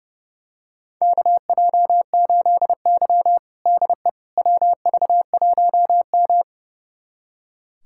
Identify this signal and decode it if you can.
Morse code, heard as a keyed tone, about 30 wpm, 700 Hz. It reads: KJ8Y DE WV1M
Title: USA Callsign DE Callsign - 4 Characters - 30wpm